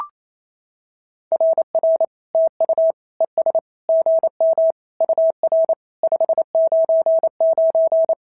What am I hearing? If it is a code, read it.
Morse code, heard as a keyed tone, about 28 wpm, 650 Hz. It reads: RR TU ES GM UR 599
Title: SOTA QSO Practice 28wpm